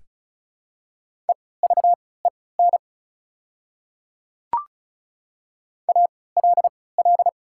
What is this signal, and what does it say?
Morse code, heard as a keyed tone, about 34 wpm, spaced as for 20 wpm, 690 Hz. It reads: EVEN ALL
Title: 3x Character Spacing - Top 200 Words - 35wpm